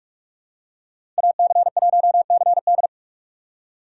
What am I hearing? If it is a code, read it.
Morse code, heard as a keyed tone, about 43 wpm, 700 Hz. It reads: AC1CD
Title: USA Call Signs - 5 Characters - 2x Character Spacing - 45wpm